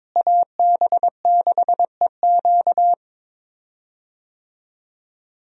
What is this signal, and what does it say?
Morse code, heard as a keyed tone, about 22 wpm, 700 Hz. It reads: AB6EQ
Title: USA Call Signs - 5 Characters - Repeated 3x - 22wpm